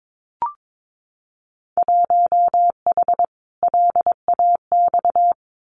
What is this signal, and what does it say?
Morse code, heard as a keyed tone, about 22 wpm, 700 Hz. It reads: 1H LAX
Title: Winter Field Day Exchanges 22wpm